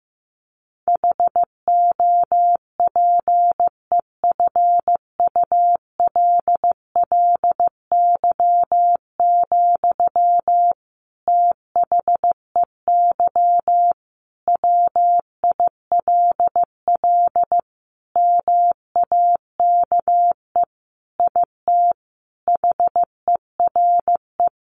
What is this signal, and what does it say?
Morse code, heard as a keyed tone, about 14 wpm, 700 Hz. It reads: HOPEFULLY, THEY WILL MAKE IT HERE
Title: Sentences from Top 1500 Words 15wpm